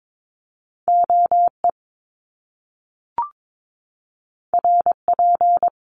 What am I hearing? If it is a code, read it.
Morse code, heard as a keyed tone, about 22 wpm, 700 Hz. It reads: OE RP
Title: Top Two Letter Combinations in Words 22wpm